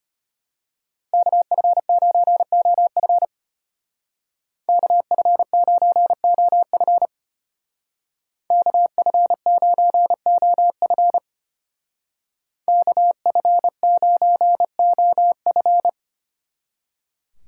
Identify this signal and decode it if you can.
Morse code, heard as a keyed tone, about 30 wpm, 700 Hz. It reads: KF9OF KF9OF KF9OF KF9OF
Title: Speed Racing - USA 2x2 Call Signs 25wpm